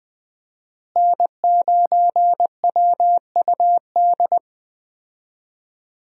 Morse code, heard as a keyed tone, about 20 wpm, 700 Hz. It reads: N9WUD
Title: USA Call Signs - 5 Characters - Repeated 3x - 20wpm